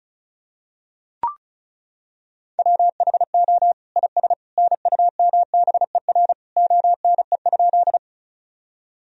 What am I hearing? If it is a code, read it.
Morse code, heard as a keyed tone, about 34 wpm, 700 Hz. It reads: WHO IS NUMBER ONE?
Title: Sentences from Top 1000 Words - Review Part 1 - 35wpm